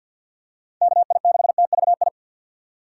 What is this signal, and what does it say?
Morse code, heard as a keyed tone, about 53 wpm, 700 Hz. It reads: KI6TVI
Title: USA Call Signs - 6 Characters - 8x Character Spacing - 50wpm